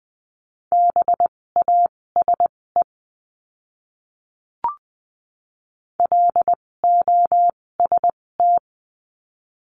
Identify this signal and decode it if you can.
Morse code, heard as a keyed tone, about 20 wpm, spaced as for 16 wpm, 700 Hz. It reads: BASE LOST
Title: 256 Lesson 42a - Words 12-20wpm